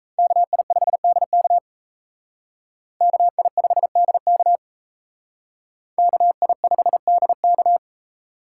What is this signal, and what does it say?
Morse code, heard as a keyed tone, about 36 wpm, 700 Hz. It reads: KI5DK KI5DK KI5DK